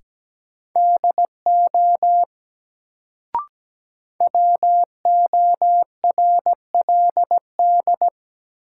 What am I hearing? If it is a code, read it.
Morse code, heard as a keyed tone, about 17 wpm, 700 Hz. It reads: DO WORLD
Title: Sets of 2 Words - New Word Drill - 17wpm